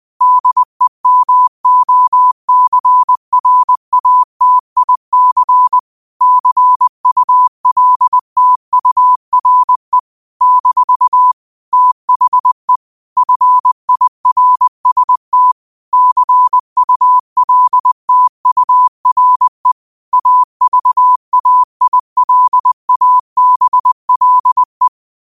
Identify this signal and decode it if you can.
Morse code, heard as a keyed tone, about 20 wpm, 1000 Hz. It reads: DEMOCRATIC CULTURE - THE FIRST CULTURE AVAILABLE